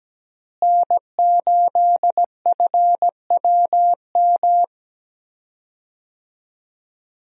Morse code, heard as a keyed tone, about 17 wpm, 700 Hz. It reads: N8FWM
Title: USA Call Signs - 5 Characters - Repeated 2x - 17wpm